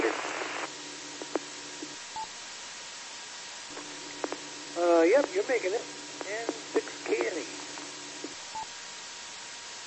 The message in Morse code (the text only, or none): none